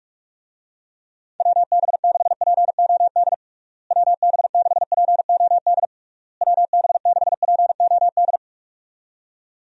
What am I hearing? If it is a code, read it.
Morse code, heard as a keyed tone, about 46 wpm, 700 Hz. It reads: WB6POD WB6POD WB6POD
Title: USA Call Signs - 6 Characters - Repeated 3x - 45wpm